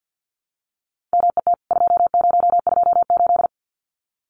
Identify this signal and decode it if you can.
Morse code, heard as a keyed tone, about 50 wpm, 700 Hz. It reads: MET 2028
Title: CWT Exchange 50wpm